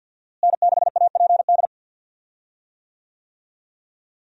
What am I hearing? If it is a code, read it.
Morse code, heard as a keyed tone, about 51 wpm, 700 Hz. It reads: N6APD
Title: USA Call Signs - 5 Characters - Repeated 2x - 50wpm